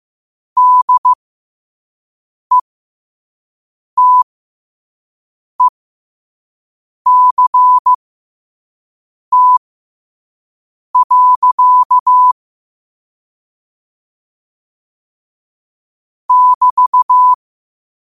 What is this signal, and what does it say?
Morse code, heard as a keyed tone, about 15 wpm, spaced as for 5 wpm, 1000 Hz. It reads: DETECT. <BT>
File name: qotdfiles / 5wpm